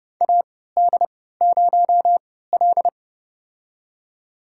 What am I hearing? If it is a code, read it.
Morse code, heard as a keyed tone, about 30 wpm, spaced as for 17 wpm, 700 Hz. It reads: AD0L
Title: USA Call Signs - 4 Characters - 3x Character Spacing - 30wpm